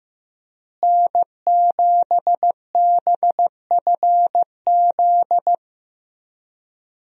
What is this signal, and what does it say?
Morse code, heard as a keyed tone, about 15 wpm, 700 Hz. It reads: N7BFZ
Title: USA Call Signs - 5 Characters - Repeated 2x - 15wpm